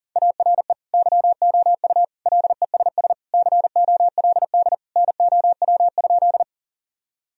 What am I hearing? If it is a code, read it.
Morse code, heard as a keyed tone, about 40 wpm, 700 Hz. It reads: ARE YOU LESS COLD NOW?
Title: Sentences from Top 1000 Words - Review Part 2 - 40wpm